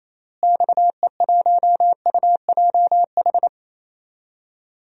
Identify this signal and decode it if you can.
Morse code, heard as a keyed tone, about 28 wpm, 700 Hz. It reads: XE1UJH
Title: DXCC-100 Call Signs - 6 Characters - Repeated 3x - 28wpm